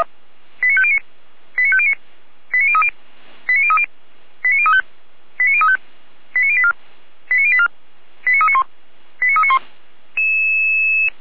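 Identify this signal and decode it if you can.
Five-tone selective call (ZVEI), heard as quick runs of five tones, tones of about 80 ms. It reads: 89509 89509 89030 89030 89035 89035 89084 89084 89391 89391
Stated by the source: ZVEI five-tone selective call received on an SDR.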